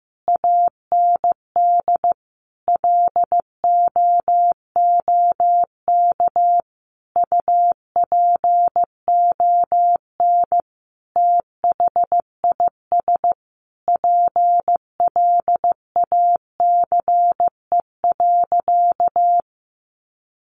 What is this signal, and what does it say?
Morse code, heard as a keyed tone, about 15 wpm, 700 Hz. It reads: AND LOOK UPON THIS PLACE.
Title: Sentences from Top 1100 Words 15wpm